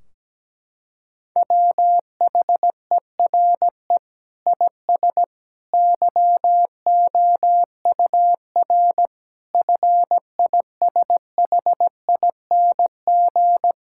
Morse code, heard as a keyed tone, about 17 wpm, 700 Hz. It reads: WHERE IS YOUR FISHING
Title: Sentences from Top 1600 Words 17wpm